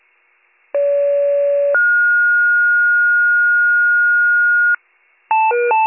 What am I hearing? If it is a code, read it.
Two-tone page, heard as a one-second tone, then a three-second tone, about 570 then 1430 Hz. It is heard from 0.7 to 4.8 s.